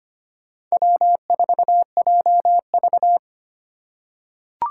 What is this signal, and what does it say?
Morse code, heard as a keyed tone, about 25 wpm, 700 Hz. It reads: W4JV
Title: USA Call Signs - 4 Characters - 3x Character Spacing - 25wpm